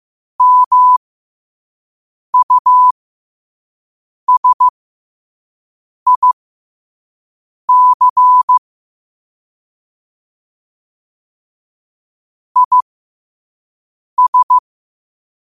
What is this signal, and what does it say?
Morse code, heard as a keyed tone, about 15 wpm, spaced as for 5 wpm, 1000 Hz. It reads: MUSIC IS
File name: qotdfiles / 5wpm